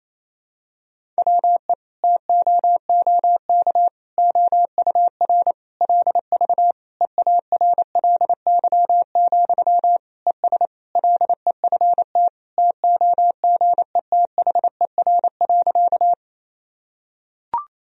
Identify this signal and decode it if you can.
Morse code, heard as a keyed tone, about 27 wpm, 700 Hz. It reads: WE TOOK OUR LV EARLY, ES LEFT TOGETHER.
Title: Sentences from Top 400 Words and Abbreviations 28wpm